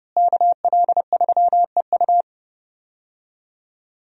Morse code, heard as a keyed tone, about 30 wpm, 700 Hz. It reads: KL3EU
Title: USA Call Signs - 5 Characters - Repeated 3x - 30wpm